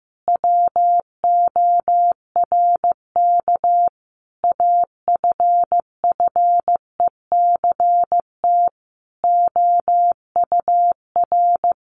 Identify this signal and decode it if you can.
Morse code, heard as a keyed tone, about 15 wpm, 700 Hz. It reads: WORK AFFECT OUR
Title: Sentences from Top 900 Words 15wpm